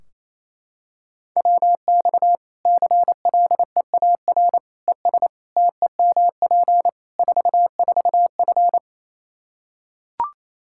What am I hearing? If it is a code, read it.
Morse code, heard as a keyed tone, about 28 wpm, 700 Hz. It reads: WX CLEAR ES TEMP 44F
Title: Practice QSO - Exchange 2 - Info - 28wpm